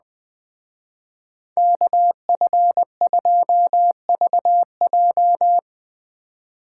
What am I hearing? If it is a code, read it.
Morse code, heard as a keyed tone, about 20 wpm, 700 Hz. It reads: KF2VJ
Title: Speed Racing - USA 2x2 Call Signs 17wpm